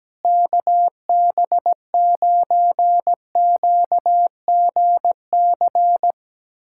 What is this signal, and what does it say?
Morse code, heard as a keyed tone, about 17 wpm, 700 Hz. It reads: KB9QGC
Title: USA Callsign DE Callsign 17wpm